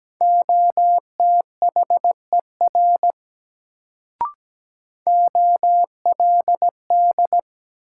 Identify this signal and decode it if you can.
Morse code, heard as a keyed tone, about 17 wpm, 700 Hz. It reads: OTHER OLD
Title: Sets of 2 Words - New Word Drill - 17wpm